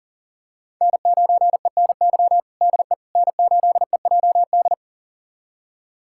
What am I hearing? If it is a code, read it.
Morse code, heard as a keyed tone, about 40 wpm, 700 Hz. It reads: N9ENY DE N8EJD
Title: USA Callsign DE Callsign - 5 Characters - 40wpm